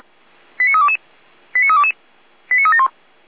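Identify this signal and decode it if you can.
Five-tone selective call (ZVEI), heard as quick runs of five tones, tones of about 70 ms. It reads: 89322 89322 89371
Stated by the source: ZVEI five-tone selective call received with an SDR.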